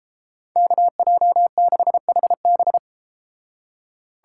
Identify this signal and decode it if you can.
Morse code, heard as a keyed tone, about 38 wpm, 700 Hz. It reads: KJ6HB